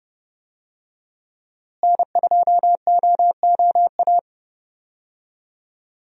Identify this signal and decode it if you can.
Morse code, heard as a keyed tone, about 30 wpm, 700 Hz. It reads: N2OOA
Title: USA Call Signs - 5 Characters - Repeated 2x - 30wpm